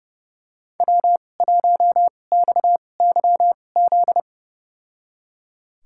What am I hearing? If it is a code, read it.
Morse code, heard as a keyed tone, about 30 wpm, spaced as for 22 wpm, 700 Hz. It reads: W1XYZ
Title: USA Call Signs - 5 Characters - 2x Character Spacing - 30wpm